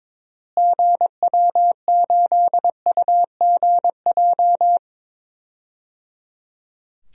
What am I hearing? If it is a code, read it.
Morse code, heard as a keyed tone, about 22 wpm, 700 Hz. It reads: GW8UGJ
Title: DXCC-100 Call Signs - 6 Characters - Repeated 3x - 22wpm